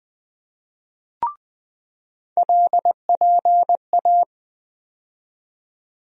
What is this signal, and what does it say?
Morse code, heard as a keyed tone, about 20 wpm, 700 Hz. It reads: LPA
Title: USA Call Sign Postfixes - 3 Letters 20wpm